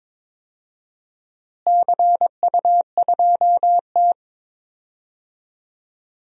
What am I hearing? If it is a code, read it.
Morse code, heard as a keyed tone, about 21 wpm, 700 Hz. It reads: CU2T
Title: DXCC-100 Call Signs - 4 Characters - Repeated 3x - 22wpm